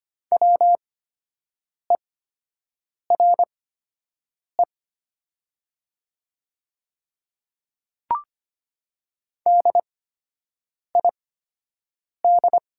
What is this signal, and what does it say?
Morse code, heard as a keyed tone, about 25 wpm, spaced as for 7 wpm, 700 Hz. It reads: WERE DID
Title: 8x Character Spacing - Top 500 Words - 25wpm